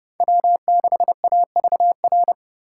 Morse code, heard as a keyed tone, about 39 wpm, 700 Hz. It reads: W6AVR